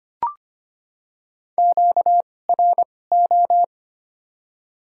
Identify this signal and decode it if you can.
Morse code, heard as a keyed tone, about 25 wpm, spaced as for 18 wpm, 700 Hz. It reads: QRO
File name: Top 90 QSO Elements - 2x Character Spacing - 25wpm